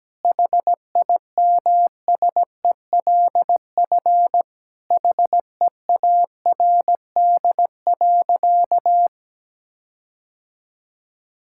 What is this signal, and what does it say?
Morse code, heard as a keyed tone, about 17 wpm, 700 Hz. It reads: HIMSELF HEARD.
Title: Sentences from Top 1600 Words 17wpm